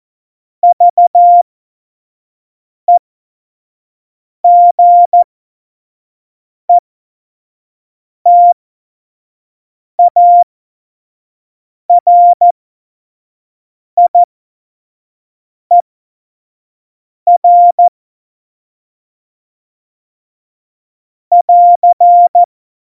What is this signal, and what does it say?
Morse code, heard as a keyed tone, about 14 wpm, spaced as for 5 wpm, 700 Hz. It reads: VEGETARIER <AR>
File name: Morsekurs / Witze / Witz 05. Januar 2021, 25 F70